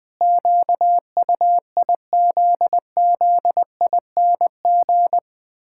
Morse code, heard as a keyed tone, about 20 wpm, 700 Hz. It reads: QUIZZING
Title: Pangram Challenge v2 - 20wpm